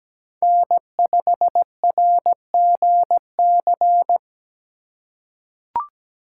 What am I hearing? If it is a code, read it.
Morse code, heard as a keyed tone, about 17 wpm, 700 Hz. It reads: N5RGC
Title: USA Call Signs - 5 Characters - 3x Character Spacing - 17wpm